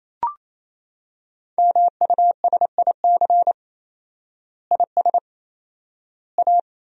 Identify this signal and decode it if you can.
Morse code, heard as a keyed tone, about 28 wpm, 700 Hz. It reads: MUSIC IS A
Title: 4x Word-Spacing - Sentences from Top 300 Words - 28wpm